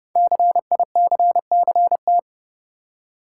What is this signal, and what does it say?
Morse code, heard as a keyed tone, about 30 wpm, 700 Hz. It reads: CICCT
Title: Code Groups - 5 Letters - 30wpm